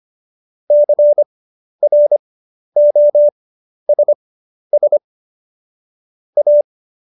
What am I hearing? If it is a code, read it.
Morse code, heard as a keyed tone, about 25 wpm, spaced as for 11 wpm, 590 Hz. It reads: CROSS A
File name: Jokes-25-char-10-eff